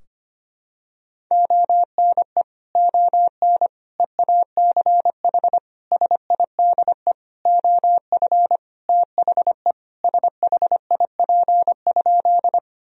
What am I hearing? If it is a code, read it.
Morse code, heard as a keyed tone, about 25 wpm, 700 Hz. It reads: ONE ON EACH SIDE OF THE SHIP?
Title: Sentences from Top 1000 Words - Review Part 2 - 25wpm